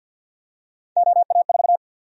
Morse code, heard as a keyed tone, about 52 wpm, 700 Hz. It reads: OA4
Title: USA Call Sign Prefixes - Repeated 2x - 50wpm